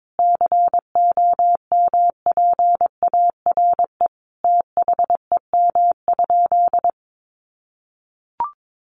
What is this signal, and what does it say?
Morse code, heard as a keyed tone, about 22 wpm, 700 Hz. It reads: COMPARE THEM?
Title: Sentences from Top 1300 Words 22wpm